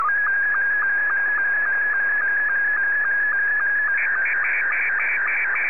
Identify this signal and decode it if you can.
SSTV (slow-scan television), Scottie 2.